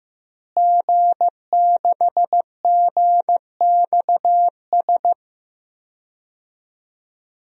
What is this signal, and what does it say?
Morse code, heard as a keyed tone, about 15 wpm, 700 Hz. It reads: G6GXS
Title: DXCC-100 Call Signs - 5 Characters - Repeated 2x - 15wpm